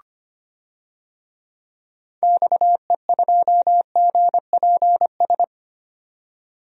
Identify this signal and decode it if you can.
Morse code, heard as a keyed tone, about 25 wpm, 700 Hz. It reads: XE2GPS
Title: DXCC-100 Call Signs - 6 Characters - 25wpm